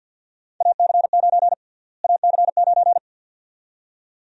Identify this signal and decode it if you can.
Morse code, heard as a keyed tone, about 52 wpm, 700 Hz. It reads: AC9 AC9
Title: USA Call Sign Prefixes - Repeated 2x - 50wpm